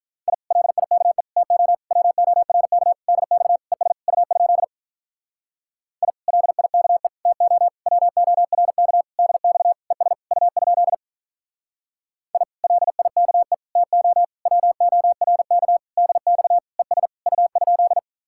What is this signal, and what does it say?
Morse code, heard as a keyed tone, about 48 wpm, 700 Hz. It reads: I LIKE TO WORK DX ES U? I LIKE TO WORK DX ES U? I LIKE TO WORK DX ES U?